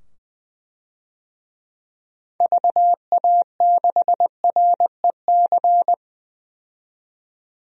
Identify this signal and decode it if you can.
Morse code, heard as a keyed tone, about 20 wpm, 700 Hz. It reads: VA6REC
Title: DXCC-100 Call Signs - 6 Characters - 20wpm